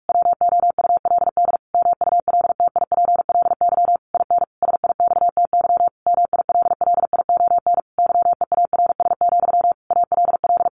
Morse code, heard as a keyed tone, about 44 wpm, 700 Hz. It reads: WOULD MULTIPLY IN SIXTY MILLION YEARS, ALL
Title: A Sound Of Thunder 45wpm